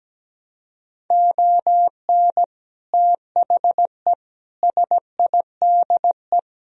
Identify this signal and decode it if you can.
Morse code, heard as a keyed tone, about 17 wpm, 700 Hz. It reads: ON THE SIDE
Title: Speed-Racing - Sets of 5 Words 17wpm